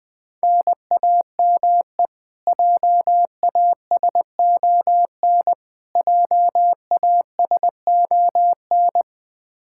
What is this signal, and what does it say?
Morse code, heard as a keyed tone, about 20 wpm, 700 Hz. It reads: NAME JASON JASON
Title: K1USN SST QSO Practice 20wpm